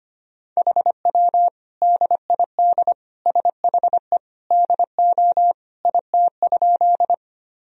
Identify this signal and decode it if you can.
Morse code, heard as a keyed tone, about 25 wpm, 700 Hz. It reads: HW DID SHE DO IT?
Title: Sentences from Top 100 Words and Abbreviations 25wpm